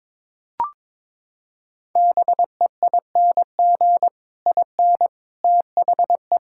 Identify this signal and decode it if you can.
Morse code, heard as a keyed tone, about 22 wpm, 700 Hz. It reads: BEING IN THE
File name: Sets of 9 Words - 22wpm